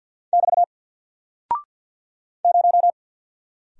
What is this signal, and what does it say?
Morse code, heard as a keyed tone, about 48 wpm, 700 Hz. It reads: <BT> 0
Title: Single Character (Letter:Number:Punctuation:Prosign) 50wpm